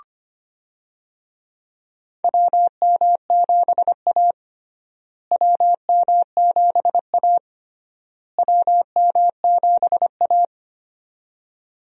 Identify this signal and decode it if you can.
Morse code, heard as a keyed tone, about 25 wpm, 700 Hz. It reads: WM7A WM7A WM7A
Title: USA Call Signs - 4 Characters - Repeated 3x - 25wpm